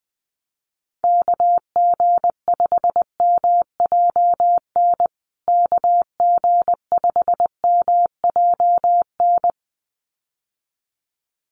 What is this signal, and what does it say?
Morse code, heard as a keyed tone, about 20 wpm, 700 Hz. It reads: KG5MJN KG5MJN
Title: USA Call Signs Repeated 2x 20wpm